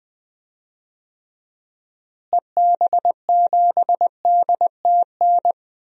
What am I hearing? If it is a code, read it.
Morse code, heard as a keyed tone, about 20 wpm, 700 Hz. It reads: EB7DTN
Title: DXCC-100 Call Signs - 6 Characters - Repeated 2x - 20wpm